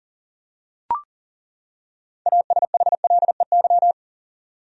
Morse code, heard as a keyed tone, about 39 wpm, 700 Hz. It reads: ASHLEY
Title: USA Common Names - 40wpm